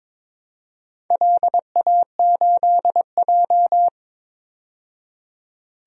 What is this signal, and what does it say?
Morse code, heard as a keyed tone, about 22 wpm, 700 Hz. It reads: LA8J
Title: DXCC-100 Call Signs - 4 Characters - 22wpm